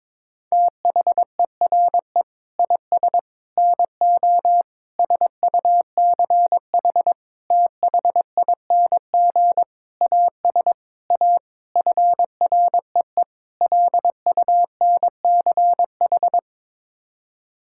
Morse code, heard as a keyed tone, about 22 wpm, 700 Hz. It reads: THERE IS NO SUCH THING AS A FREE LUNCH
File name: Sets of 9 Words - 22wpm